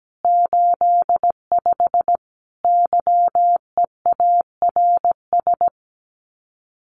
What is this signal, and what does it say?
Morse code, heard as a keyed tone, about 17 wpm, 700 Hz. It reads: 85 YEARS
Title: Common QSO Phrases 17wpm